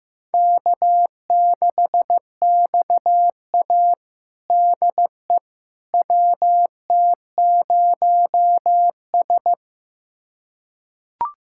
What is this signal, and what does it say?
Morse code, heard as a keyed tone, about 16 wpm, 700 Hz. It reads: K6XA DE WT0S
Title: USA Callsign DE Callsign - 4 Characters - 15wpm